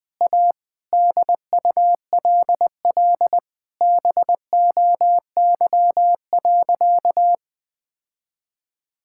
Morse code, heard as a keyed tone, about 19 wpm, 700 Hz. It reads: A DULL BOY.
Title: Sentences with 8-Word Phrases - 20wpm